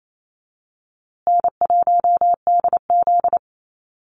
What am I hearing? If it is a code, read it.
Morse code, heard as a keyed tone, about 28 wpm, 700 Hz. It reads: N1DZ
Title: USA Call Signs Repeated 2x 28wpm